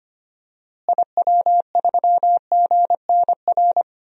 Morse code, heard as a keyed tone, about 25 wpm, 700 Hz. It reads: IW3GNR
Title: DXCC-100 Call Signs - 6 Characters - Repeated 2x - 25wpm